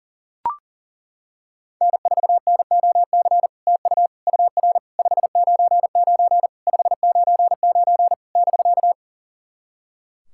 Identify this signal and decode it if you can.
Morse code, heard as a keyed tone, about 40 wpm, 690 Hz. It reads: N4NOC TU UR 599 599 <BK>
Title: SOTA QSO Practice 40wpm